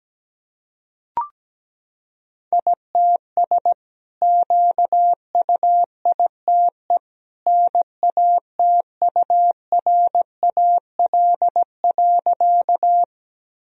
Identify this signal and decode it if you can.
Morse code, heard as a keyed tone, about 17 wpm, 700 Hz. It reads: ITS QUITE NATURAL.
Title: Sentences from Top 1000 Words - Review Part 4 - 17wpm